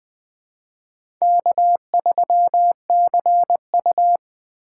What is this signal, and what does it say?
Morse code, heard as a keyed tone, about 21 wpm, 700 Hz. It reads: K3CU
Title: 2023 POTA Hunter Call Signs - 20wpm